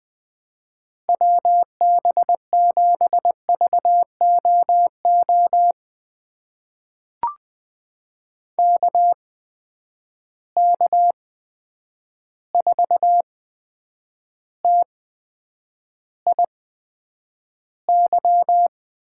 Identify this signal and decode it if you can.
Morse code, heard as a keyed tone, about 20 wpm, spaced as for 9 wpm, 700 Hz. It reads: WB7VOO KK4TIY
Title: USA Call Signs - 6 Characters - 8x Character Spacing - 20wpm